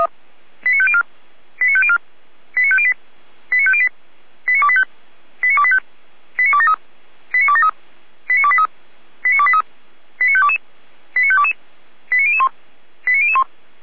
Five-tone selective call (ZVEI), heard as quick runs of five tones, tones of about 70 ms. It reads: 89504 89504 89508 89508 89285 89285 89273 89273 89283 89283 89522 89522 89001 89001